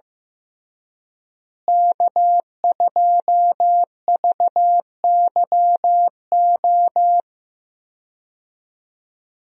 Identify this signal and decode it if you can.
Morse code, heard as a keyed tone, about 15 wpm, 700 Hz. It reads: K2VYO
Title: USA Call Signs - 5 Characters - Repeated 3x - 15wpm